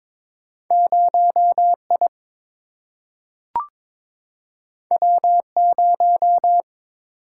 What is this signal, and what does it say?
Morse code, heard as a keyed tone, about 22 wpm, 700 Hz. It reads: 0 I W 0
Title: Single-Letter Single-Number 22wpm